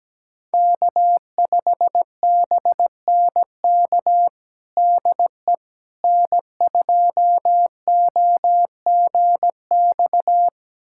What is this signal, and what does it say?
Morse code, heard as a keyed tone, about 17 wpm, 700 Hz. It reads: K5BNK DE N2OGX
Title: USA Callsign DE Callsign - 5 Characters - 17wpm